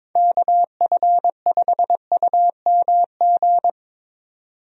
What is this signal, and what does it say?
Morse code, heard as a keyed tone, about 22 wpm, 700 Hz. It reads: KF5UMG
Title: USA Call Signs - 6 Characters - 22wpm